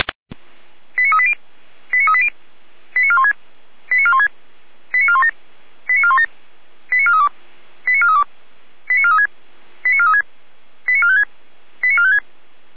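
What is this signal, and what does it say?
Five-tone selective call (ZVEI), heard as quick runs of five tones, tones of about 70 ms. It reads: 89380 89380 89416 89416 89417 89417 89432 89432 89436 89436 89456 89456
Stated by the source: ZVEI five-tone selective call received on SDR.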